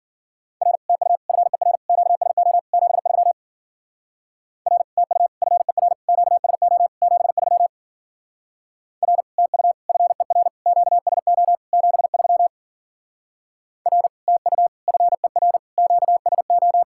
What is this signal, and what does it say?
Morse code, heard as a keyed tone, about 48 wpm, 700 Hz. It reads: R TU FER QSO 73 R TU FER QSO 73 R TU FER QSO 73 R TU FER QSO
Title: Speed-Racing Common QSO Phrases Encore - 40wpm